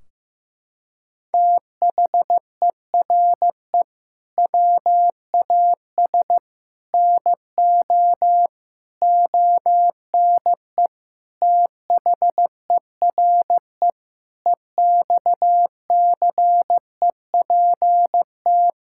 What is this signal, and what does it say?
Morse code, heard as a keyed tone, about 15 wpm, 700 Hz. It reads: THERE WAS NO ONE THERE EXCEPT
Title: Sentences from Top 1600 Words 15wpm